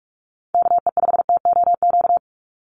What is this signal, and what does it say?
Morse code, heard as a keyed tone, about 44 wpm, 700 Hz. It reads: KE5TOQ
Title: USA Call Signs 45wpm